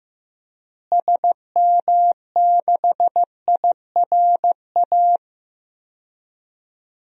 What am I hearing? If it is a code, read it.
Morse code, heard as a keyed tone, about 15 wpm, 700 Hz. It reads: SM6IRA
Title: DXCC-100 Call Signs - 6 Characters - Repeated 3x - 15wpm